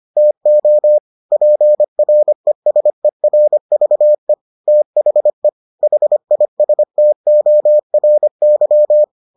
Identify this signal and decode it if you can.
Morse code, heard as a keyed tone, about 25 wpm, 600 Hz. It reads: TO PRESERVE THE HISTORY